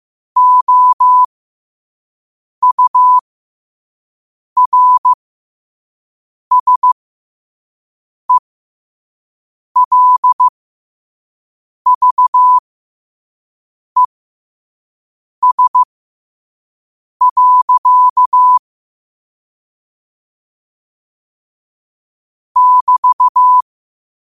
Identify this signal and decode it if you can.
Morse code, heard as a keyed tone, about 15 wpm, spaced as for 5 wpm, 1000 Hz. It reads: OURSELVES. <BT>
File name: 5wpm